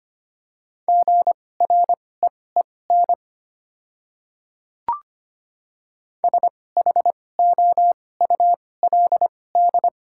Morse code, heard as a keyed tone, about 25 wpm, spaced as for 18 wpm, 700 Hz. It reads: GREEN SHOULD
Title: 2x Character Spacing - Top 500 Words - 25wpm